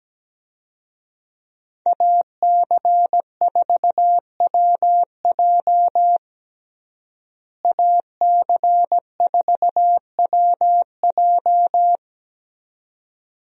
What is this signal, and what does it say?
Morse code, heard as a keyed tone, about 17 wpm, 700 Hz. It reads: AC4WJ AC4WJ
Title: USA Call Signs - 5 Characters - Repeated 2x - 17wpm